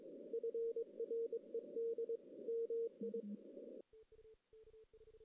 Morse code, heard as a keyed tone, about 24 wpm, 440 Hz. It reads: FRED MI
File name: 2016 JAN CW NAQP / 2016 Jan naqp k5ot - 0-2